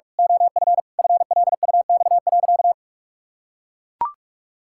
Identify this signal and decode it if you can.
Morse code, heard as a keyed tone, about 45 wpm, 700 Hz. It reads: OF FLUX.
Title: Pangram Challenge v2 - 45wpm